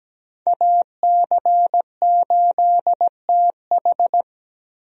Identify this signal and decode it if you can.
Morse code, heard as a keyed tone, about 17 wpm, 700 Hz. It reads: AC8TH
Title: USA Call Signs - 5 Characters - 17wpm